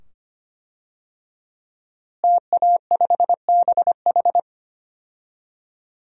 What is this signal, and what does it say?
Morse code, heard as a keyed tone, about 25 wpm, 700 Hz. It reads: TA5BH
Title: DXCC-100 Call Signs - 5 Characters - Repeated 2x - 25wpm